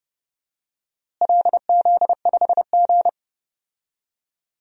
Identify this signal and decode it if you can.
Morse code, heard as a keyed tone, about 30 wpm, 700 Hz. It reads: LZ5G
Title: DXCC-100 Call Signs - 4 Characters - Repeated 3x - 30wpm